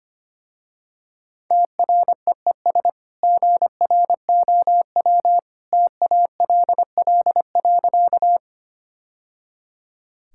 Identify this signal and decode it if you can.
Morse code, heard as a keyed tone, about 25 wpm, 690 Hz. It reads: TREES GROW TALL.
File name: Easy Sentences with Hints - 25wpm